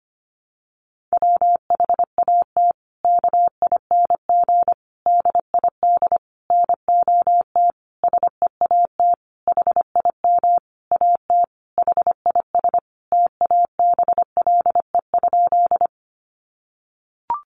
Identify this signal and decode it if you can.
Morse code, heard as a keyed tone, about 25 wpm, 700 Hz. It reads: WHAT KING DID NOT SEAT HIM AT HIS TABLE?
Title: Sentences from Top 1500 Words 25wpm